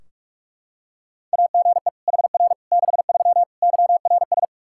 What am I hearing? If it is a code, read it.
Morse code, heard as a keyed tone, about 41 wpm, 700 Hz. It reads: AGE HR 63 YRS
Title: Practice QSO - Exchange 3 - More Info - 45wpm